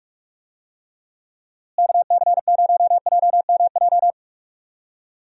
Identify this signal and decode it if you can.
Morse code, heard as a keyed tone, about 41 wpm, 700 Hz. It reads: KC0JMJ
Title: USA Call Signs - 6 Characters - Repeated 3x - 45wpm